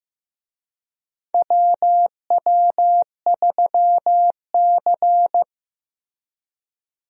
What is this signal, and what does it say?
Morse code, heard as a keyed tone, about 15 wpm, 700 Hz. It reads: WW3C
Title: USA Call Signs - 4 Characters - Repeated 3x - 15wpm